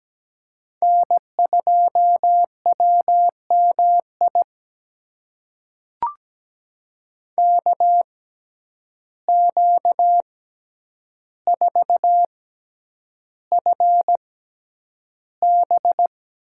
Morse code, heard as a keyed tone, about 17 wpm, spaced as for 9 wpm, 700 Hz. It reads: N2WMI KQ4FB
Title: USA Call Signs - 5 Characters - 6x Character Spacing - 17wpm